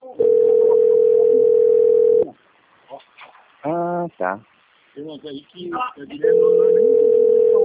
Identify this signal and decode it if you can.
Ringback tone (North America), 440 + 480 Hz, with speech.
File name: out- - - - - .5816